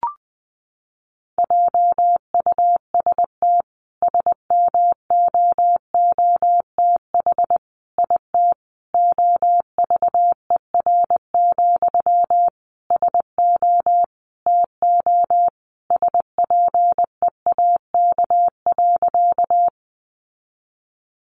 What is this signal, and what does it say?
Morse code, heard as a keyed tone, about 20 wpm, 700 Hz. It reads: JUST SMOOTH IT OVER, SO TO SPEAK.
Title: Sentences from Top 1400 Words 20wpm